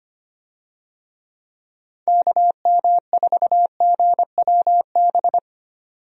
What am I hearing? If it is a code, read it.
Morse code, heard as a keyed tone, about 25 wpm, 700 Hz. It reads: KM4GWB
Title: USA Call Signs - 6 Characters - 25wpm